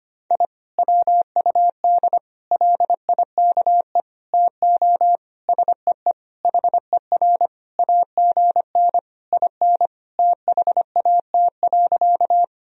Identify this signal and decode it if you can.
Morse code, heard as a keyed tone, about 25 wpm, 700 Hz. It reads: I WUD LIKE TO SEE HER AGN IN THAT.
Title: Sentences from Top 200 Words and Abbreviations 25wpm